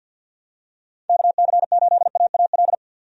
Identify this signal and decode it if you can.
Morse code, heard as a keyed tone, about 50 wpm, 700 Hz. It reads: KC8AAL
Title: USA Call Signs - 6 Characters - 50wpm